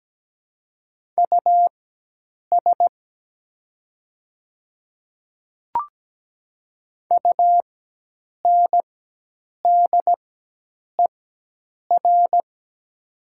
Morse code, heard as a keyed tone, about 17 wpm, spaced as for 8 wpm, 700 Hz. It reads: US UNDER
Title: 4x Character Spacing - Top 200 Words - 17wpm